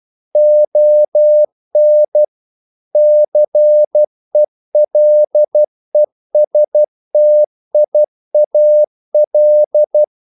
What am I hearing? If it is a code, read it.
Morse code, heard as a keyed tone, about 12 wpm, 600 Hz. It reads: ON CELESTIAL
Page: English quotations in CW (telegraphy - morse)